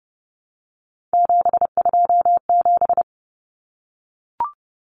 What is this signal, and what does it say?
Morse code, heard as a keyed tone, about 30 wpm, 700 Hz. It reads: 727
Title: Numbers - 3-Digit 30wpm